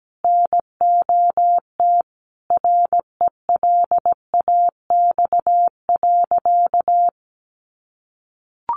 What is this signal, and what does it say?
Morse code, heard as a keyed tone, about 17 wpm, 700 Hz. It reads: NOT RELAX.
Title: Sentences from Top 1400 Words 17wpm